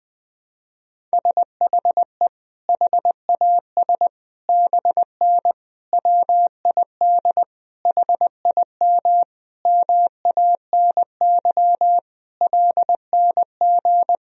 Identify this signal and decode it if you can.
Morse code, heard as a keyed tone, about 20 wpm, 700 Hz. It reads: SHE HAS BN WID HIM MANY LNG
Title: Sentences from Top 200 Words and Abbreviations 20wpm